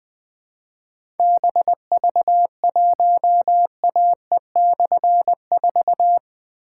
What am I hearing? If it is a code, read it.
Morse code, heard as a keyed tone, about 20 wpm, 700 Hz. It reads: BV1AE/4
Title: Difficult Call Signs 20wpm